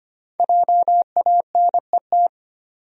Morse code, heard as a keyed tone, about 25 wpm, 700 Hz. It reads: JANET
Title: USA Common Names - 25wpm